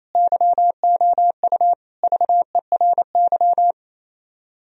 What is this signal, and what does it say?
Morse code, heard as a keyed tone, about 26 wpm, 700 Hz. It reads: YOU VERY
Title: Speed-Racing - Sets of 3 Words 28wpm